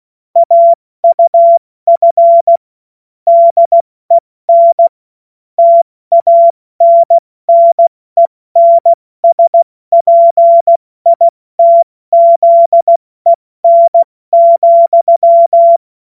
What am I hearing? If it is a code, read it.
Morse code, heard as a keyed tone, about 16 wpm, 680 Hz. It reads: AUF DEN TANNENSPITZEN,
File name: Morsekurs / Witze / Witz 31. Januar 2023 70 F80